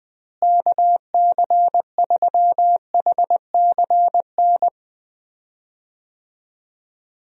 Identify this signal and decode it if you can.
Morse code, heard as a keyed tone, about 21 wpm, 700 Hz. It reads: KC3HCN
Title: USA Call Signs - 6 Characters - Repeated 2x - 20wpm